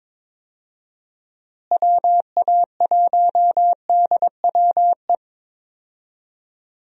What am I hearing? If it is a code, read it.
Morse code, heard as a keyed tone, about 22 wpm, 700 Hz. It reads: WA1DWE
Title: USA Call Signs - 6 Characters - Repeated 3x - 22wpm